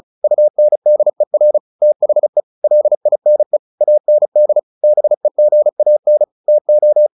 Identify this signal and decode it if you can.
Morse code, heard as a keyed tone, about 36 wpm, 600 Hz. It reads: UNDER THE LINE AND BEGAN TO
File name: the old man and the sea 2 35wpm